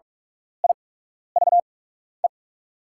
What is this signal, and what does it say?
Morse code, heard as a keyed tone, about 44 wpm, spaced as for 12 wpm, 700 Hz. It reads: IVE
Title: 8x Character Spacing - Top 500 Words - 45wpm